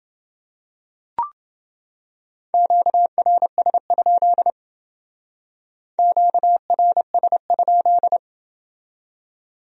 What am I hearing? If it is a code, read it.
Morse code, heard as a keyed tone, about 28 wpm, 700 Hz. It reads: QRS? QRS?